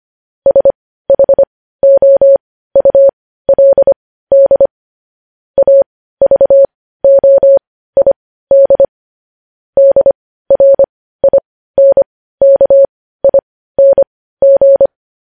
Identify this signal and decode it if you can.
Morse code, heard as a keyed tone, about 25 wpm, spaced as for 15 wpm, 550 Hz. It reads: SHOULD AVOID DRINKING
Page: news-headlines-in-morse-code-at-15-wpm